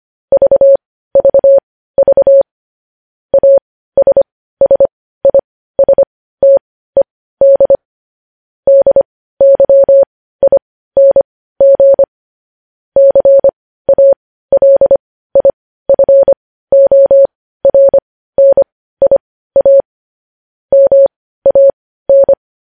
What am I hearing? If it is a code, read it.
Morse code, heard as a keyed tone, about 25 wpm, spaced as for 15 wpm, 550 Hz. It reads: VVV ASSISTED DYING CALIFORNIA MAN
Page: news headlines in morse code at 15 wpm